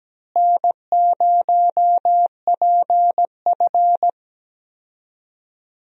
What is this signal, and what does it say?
Morse code, heard as a keyed tone, about 17 wpm, 700 Hz. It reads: N0PF
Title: USA Call Signs - 4 Characters - Repeated 2x - 17wpm